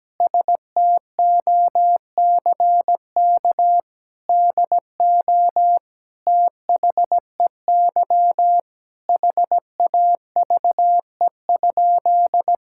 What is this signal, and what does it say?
Morse code, heard as a keyed tone, about 17 wpm, 700 Hz. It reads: STOCK DO THEY HAVE?
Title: Sentences from Top 1000 Words - Review Part 5 - 17wpm